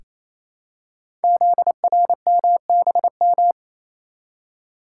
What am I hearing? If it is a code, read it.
Morse code, heard as a keyed tone, about 29 wpm, 700 Hz. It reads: ZRMBM
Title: Code Groups - 5 Letters - 28wpm